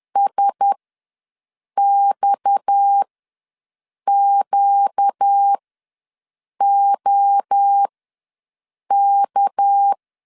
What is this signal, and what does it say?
Morse code, heard as a keyed tone, about 10 wpm, spaced as for 6 wpm, 800 Hz. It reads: SXQOK
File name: LEZIONI / 6 PKX0 / 6 PKX ws 10wpm 3706